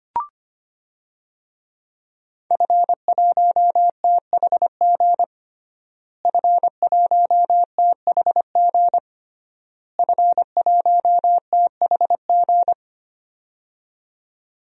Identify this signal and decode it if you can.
Morse code, heard as a keyed tone, about 25 wpm, 700 Hz. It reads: F1THG F1THG F1THG
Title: DXCC-100 Call Signs - 5 Characters - Repeated 3x - 25wpm